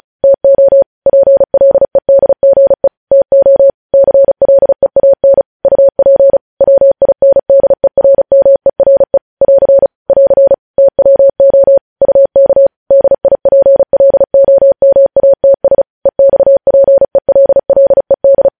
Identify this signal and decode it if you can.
Morse code, heard as a keyed tone, about 36 wpm, 550 Hz. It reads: TO PLEDGE TO CLEAN UP WINDERMERE <AR> <AR> TWO UK DIPLOMATS EXPELLED
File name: podcast / News headlines-1x-35x35